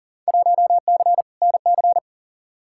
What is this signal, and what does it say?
Morse code, heard as a keyed tone, about 40 wpm, 700 Hz. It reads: 1C NC
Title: Field Day Exchanges - 40wpm